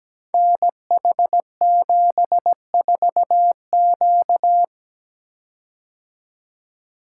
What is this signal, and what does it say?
Morse code, heard as a keyed tone, about 17 wpm, 700 Hz. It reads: NH74Q
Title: Code Groups - 5 Letter:Numbers - 17wpm